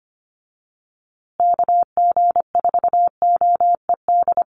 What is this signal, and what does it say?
Morse code, heard as a keyed tone, about 25 wpm, 700 Hz. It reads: KG4OED
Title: USA Call Signs Repeated 3x 25wpm